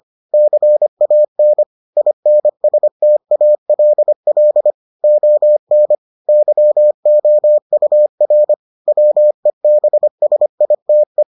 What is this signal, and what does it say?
Morse code, heard as a keyed tone, about 25 wpm, 600 Hz. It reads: CAN INSTALL ON YOUR WEBSITE